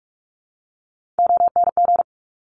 Morse code, heard as a keyed tone, about 46 wpm, 700 Hz. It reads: ONG
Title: Top Three Letter Combinations in Words 45wpm